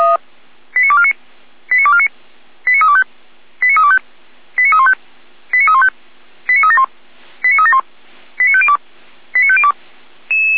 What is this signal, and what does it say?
Five-tone selective call (ZVEI), heard as quick runs of five tones, tones of about 70 ms. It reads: 89249 89249 89325 89325 89315 89315 89371 89371 89502 89502